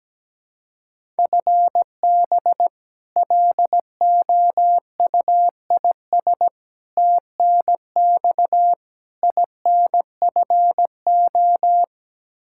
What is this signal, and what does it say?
Morse code, heard as a keyed tone, about 17 wpm, 700 Hz. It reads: FB LOUIS TNX INFO
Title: K1USN SST QSO Practice 17wpm